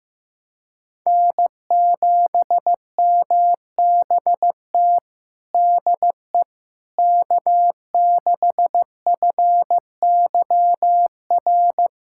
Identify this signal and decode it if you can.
Morse code, heard as a keyed tone, about 15 wpm, 700 Hz. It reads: N7MBT DE K6FYR
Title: USA Callsign DE Callsign - 5 Characters - 15wpm